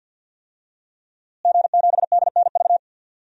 Morse code, heard as a keyed tone, about 46 wpm, 700 Hz. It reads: G7DNV
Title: DXCC-100 Call Signs - 5 Characters - Repeated 3x - 50wpm